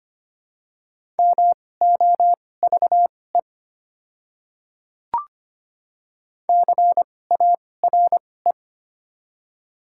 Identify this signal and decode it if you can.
Morse code, heard as a keyed tone, about 25 wpm, spaced as for 18 wpm, 700 Hz. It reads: MOVE CARE
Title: 2x Character Spacing - Top 300 Words - 25wpm